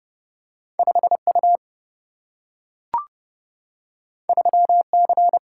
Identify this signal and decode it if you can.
Morse code, heard as a keyed tone, about 30 wpm, 700 Hz. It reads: 5 U 3 C
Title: Single-Letter Single-Number 30wpm